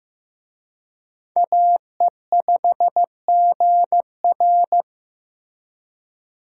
Morse code, heard as a keyed tone, about 18 wpm, 700 Hz. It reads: AE5GR
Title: Speed Racing - USA 2x2 Call Signs 15wpm